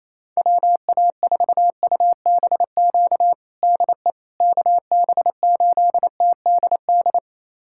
Morse code, heard as a keyed tone, about 28 wpm, 700 Hz. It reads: WA4UBQ DE KB8TDD
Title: USA Callsign DE Callsign - 6 Characters - 28wpm